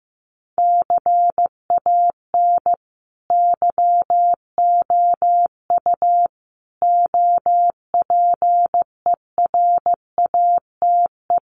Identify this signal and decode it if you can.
Morse code, heard as a keyed tone, about 15 wpm, 700 Hz. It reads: CAN YOU OPERATE
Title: Sentences from Top 1500 Words 15wpm